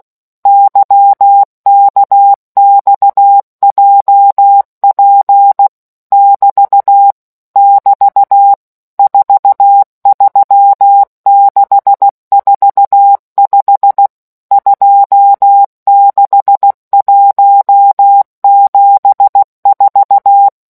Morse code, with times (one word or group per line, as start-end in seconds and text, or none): 0.5-5.7 YKXJP
6.1-7.1 <BT>
7.6-8.5 <BT>
9.0-14.1 43645
14.5-20.5 26174